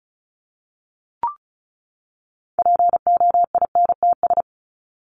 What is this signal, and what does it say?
Morse code, heard as a keyed tone, about 34 wpm, 700 Hz. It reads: POINTS
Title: ES:S Suffix - Top 100 Words with Base Words 35wpm